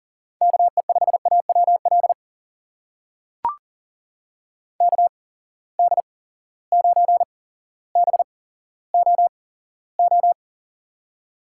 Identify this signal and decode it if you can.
Morse code, heard as a keyed tone, about 40 wpm, spaced as for 17 wpm, 700 Hz. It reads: KE5AWL KD9BOO
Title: USA Call Signs - 6 Characters - 8x Character Spacing - 40wpm